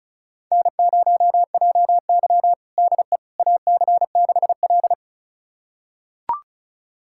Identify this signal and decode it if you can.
Morse code, heard as a keyed tone, about 36 wpm, 700 Hz. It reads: N0JY DE AC6L
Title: USA Callsign DE Callsign - 4 Characters - 35wpm